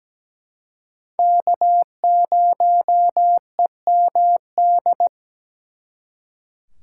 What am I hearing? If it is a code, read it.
Morse code, heard as a keyed tone, about 20 wpm, 700 Hz. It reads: K0EMD
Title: Speed Racing - USA 1x3 Call Signs 17wpm